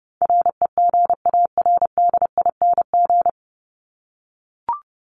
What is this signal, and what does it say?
Morse code, heard as a keyed tone, about 30 wpm, 700 Hz. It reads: REGARDING
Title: ING Suffix - Top 100 Words with Base Words 30wpm